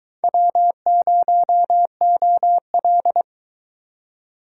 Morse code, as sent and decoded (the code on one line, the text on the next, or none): .-- ----- --- .-..
W0OL